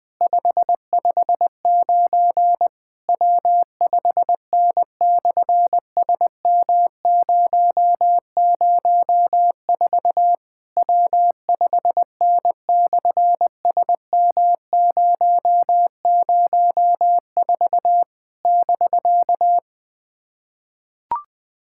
Morse code, heard as a keyed tone, about 18 wpm, 700 Hz. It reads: 559 W5N/SM004 W5N/SM004 <BK>
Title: SOTA QSO Practice 20wpm